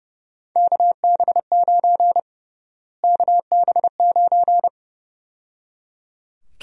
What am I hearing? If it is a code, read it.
Morse code, heard as a keyed tone, about 30 wpm, 700 Hz. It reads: KB9 KB9
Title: USA Call Sign Prefixes - Repeated 2x - 30wpm